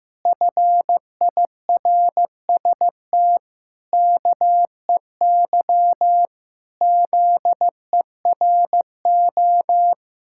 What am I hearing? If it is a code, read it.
Morse code, heard as a keyed tone, about 15 wpm, 700 Hz. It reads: FIRST KEY ZERO
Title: Pangram Challenge v2 - 15wpm